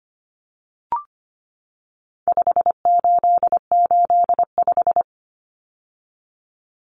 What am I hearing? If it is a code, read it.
Morse code, heard as a keyed tone, about 25 wpm, 700 Hz. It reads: 5885
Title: Numbers - 4-Digit 25wpm